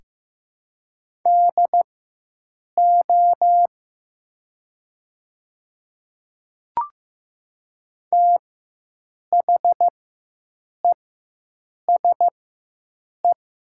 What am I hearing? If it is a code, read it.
Morse code, heard as a keyed tone, about 15 wpm, spaced as for 7 wpm, 700 Hz. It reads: DO THESE
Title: 4x Character Spacing - Top 100 Words - 15wpm